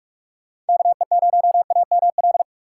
Morse code, heard as a keyed tone, about 62 wpm, spaced as for 53 wpm, 700 Hz. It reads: KE0AML
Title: Speed Racing - USA 2x3 Call Signs 45wpm